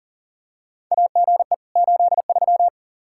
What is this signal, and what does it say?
Morse code, heard as a keyed tone, about 40 wpm, 700 Hz. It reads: AGE 83
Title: Practice QSO - Exchange 3 - More Info - 40wpm